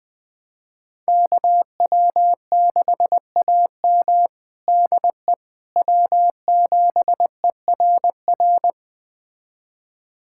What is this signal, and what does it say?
Morse code, heard as a keyed tone, about 21 wpm, 700 Hz. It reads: KW6AM DE W7ERR
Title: USA Callsign DE Callsign - 5 Characters - 20wpm